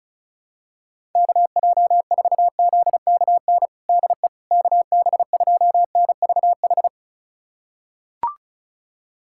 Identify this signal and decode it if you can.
Morse code, heard as a keyed tone, about 36 wpm, 700 Hz. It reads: KJ4ZKN DE KB2NVH
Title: USA Callsign DE Callsign - 6 Characters - 35wpm